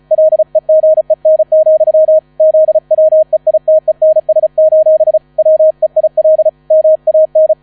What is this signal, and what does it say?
Morse code, heard as a keyed tone, about 36 wpm, 630 Hz, with radio noise. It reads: LEGEN, ZWEITENS: WEIL MAN
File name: 35wpm